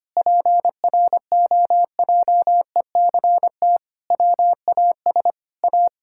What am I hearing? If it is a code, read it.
Morse code, heard as a keyed tone, about 25 wpm, 700 Hz. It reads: PROJECT WAS A
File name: practice / 25wpm / Sentences with 9-Word Phrases - 25wpm